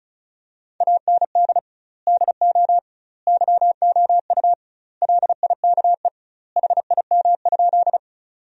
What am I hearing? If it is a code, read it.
Morse code, heard as a keyed tone, about 36 wpm, 700 Hz. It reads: AND DO YOU LIKE HIM?
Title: 2x Word-Spacing - Sentences from Top 100 Words - 35wpm